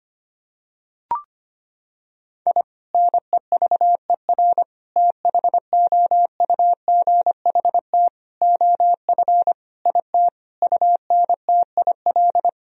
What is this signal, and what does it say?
Morse code, heard as a keyed tone, about 25 wpm, 700 Hz. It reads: I NEVER THOUGHT OF IT UNTIL
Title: Sentences from Top 1000 Words - Review Part 1 - 25wpm